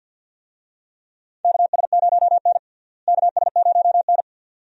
Morse code, heard as a keyed tone, about 49 wpm, 700 Hz. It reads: KS0N KS0N
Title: USA Call Signs - 4 Characters - Repeated 2x - 50wpm